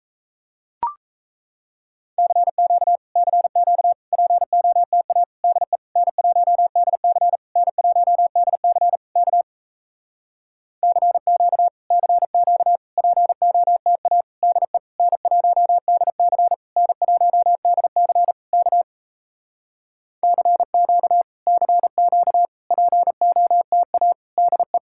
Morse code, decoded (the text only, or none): CQ CQ POTA DE N1DC N1DC K CQ CQ POTA DE N1DC N1DC K CQ CQ POTA DE